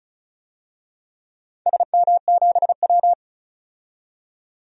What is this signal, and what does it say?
Morse code, heard as a keyed tone, about 34 wpm, 700 Hz. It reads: SM7W
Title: DXCC-100 Call Signs - 4 Characters - 35wpm